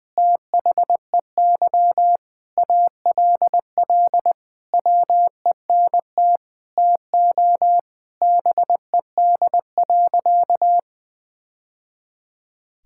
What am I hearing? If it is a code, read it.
Morse code, heard as a keyed tone, about 21 wpm, 700 Hz. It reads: THEY ALL WENT TO BED.
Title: Sentences from Top 1000 Words - Review Part 6 - 20wpm